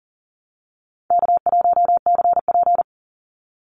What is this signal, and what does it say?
Morse code, heard as a keyed tone, about 39 wpm, 700 Hz. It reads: K1CP
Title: Call Signs - 40wpm